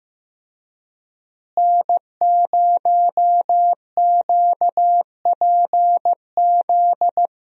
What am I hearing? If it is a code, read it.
Morse code, heard as a keyed tone, about 15 wpm, 700 Hz. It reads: N0QPZ
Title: USA Call Signs - 5 Characters - Repeated 3x - 15wpm